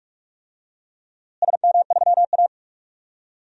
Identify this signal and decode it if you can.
Morse code, heard as a keyed tone, about 46 wpm, 700 Hz. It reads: SM3A
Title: DXCC-100 Call Signs - 4 Characters - Repeated 3x - 45wpm